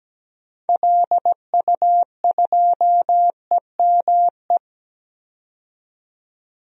Morse code, heard as a keyed tone, about 17 wpm, 700 Hz. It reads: LU2EME
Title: DXCC-100 Call Signs - 6 Characters - 17wpm